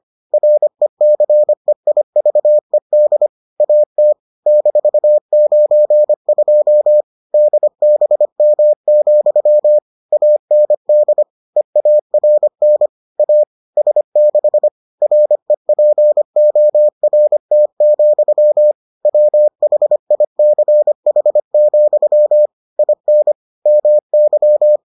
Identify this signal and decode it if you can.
Morse code, heard as a keyed tone, about 25 wpm, 600 Hz. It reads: RECEIVED AT -92 DBM, AND EARN A S6 REPORT, WHICH, IN MY